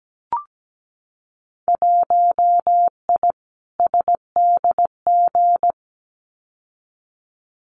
Morse code, heard as a keyed tone, about 17 wpm, 700 Hz. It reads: 1I SDG
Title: Winter Field Day Exchanges 17wpm